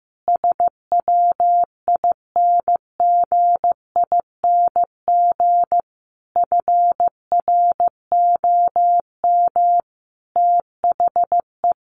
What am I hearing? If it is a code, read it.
Morse code, heard as a keyed tone, about 15 wpm, 700 Hz. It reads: SWINGING FROM THE
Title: Easy Phrases 15wpm